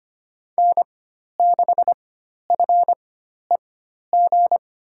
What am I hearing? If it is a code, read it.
Morse code, heard as a keyed tone, about 25 wpm, spaced as for 12 wpm, 700 Hz. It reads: N6FEG
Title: USA Call Signs - 4x Character Spacing - 25wpm